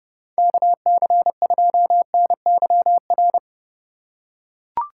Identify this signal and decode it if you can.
Morse code, heard as a keyed tone, about 30 wpm, 700 Hz. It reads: KC2NYR
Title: USA Call Signs - 6 Characters - 6x Character Spacing - 30wpm